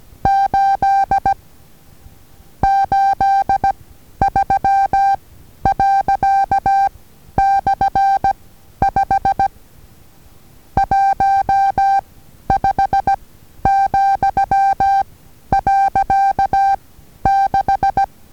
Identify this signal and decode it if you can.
Morse code, heard as a keyed tone, about 16 wpm, 790 Hz. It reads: 8 8 3 . / 5 1 5 , . 6
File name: Taviro HA6VA A